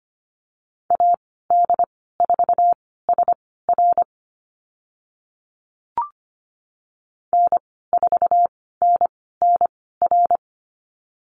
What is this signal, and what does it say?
Morse code, heard as a keyed tone, about 25 wpm, spaced as for 16 wpm, 700 Hz. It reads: AD4SR N4NNR
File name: Lesson 15g - Number 5 - Call Signs 25wpm